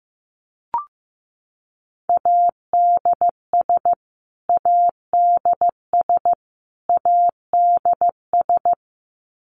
Morse code, heard as a keyed tone, about 15 wpm, 700 Hz. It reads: ADS ADS ADS
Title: Abbreviations Repeated 3x - 15wpm